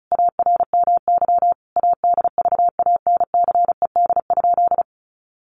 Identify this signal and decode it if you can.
Morse code, heard as a keyed tone, about 34 wpm, 700 Hz. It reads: ARMY ADVANCED?
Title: Sentences from Top 1200 Words 35wpm